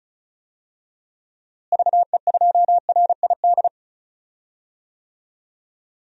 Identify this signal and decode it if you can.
Morse code, heard as a keyed tone, about 34 wpm, 700 Hz. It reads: VE2RID
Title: DXCC-100 Call Signs - 6 Characters - Repeated 3x - 35wpm